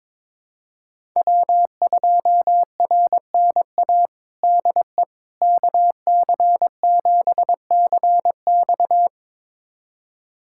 Morse code, heard as a keyed tone, about 22 wpm, 700 Hz. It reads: W2RNA DE KC7CX
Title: USA Callsign DE Callsign - 5 Characters - 22wpm